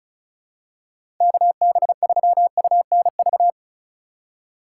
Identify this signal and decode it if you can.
Morse code, heard as a keyed tone, about 34 wpm, 700 Hz. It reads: KB3UNV
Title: USA Call Signs - 6 Characters - 3x Character Spacing - 35wpm